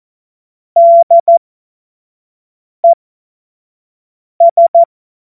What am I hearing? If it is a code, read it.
Morse code, heard as a keyed tone, about 14 wpm, spaced as for 5 wpm, 680 Hz. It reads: DES
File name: Morsekurs / Witze / Witz 03. Januar 2023 25 F70